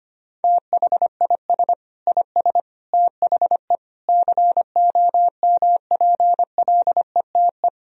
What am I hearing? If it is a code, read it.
Morse code, heard as a keyed tone, about 25 wpm, 700 Hz. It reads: THIS IS THE COMPLETE
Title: Sentences from Top 1000 Words - Review Part 2 - 25wpm